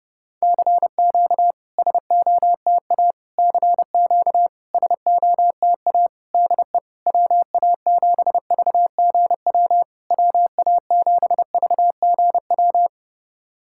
Morse code, heard as a keyed tone, about 30 wpm, 690 Hz. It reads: CQ SOTA CQ SOTA DE WA7VGW WA7VGW
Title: SOTA QSO Practice 30wpm